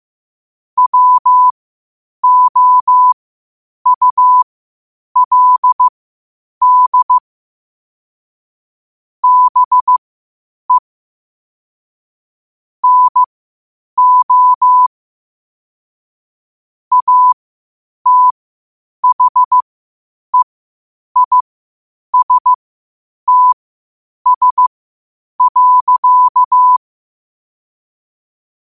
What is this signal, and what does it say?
Morse code, heard as a keyed tone, about 15 wpm, spaced as for 8 wpm, 1000 Hz. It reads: WOULD BE NO ATHEISTS.
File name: qotdfiles / 8wpm